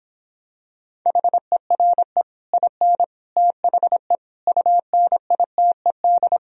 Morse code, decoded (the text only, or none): HERE IN THE UNITED